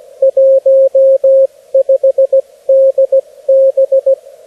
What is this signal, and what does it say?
Morse code, heard as a keyed tone, about 17 wpm, 540 Hz, with radio noise. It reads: 15DB
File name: beacons / HG1BTV